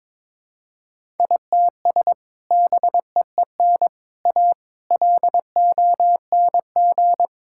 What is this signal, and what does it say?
Morse code, heard as a keyed tone, about 21 wpm, 700 Hz. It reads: ITS BEEN A LONG
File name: practice / 22wpm / Sentences with 3-Word Phrases - 22wpm